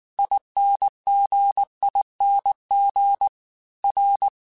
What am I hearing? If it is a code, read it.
Morse code, heard as a keyed tone, about 19 wpm, 800 Hz. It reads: INGING R
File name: Special INGs-mp3s / Special INGs-mp3s / Dble Ings sp @18